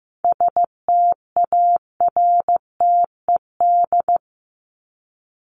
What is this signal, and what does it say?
Morse code, heard as a keyed tone, about 15 wpm, 700 Hz. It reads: STARTED
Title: ED Suffix - Top 100 Words with Base Words 15wpm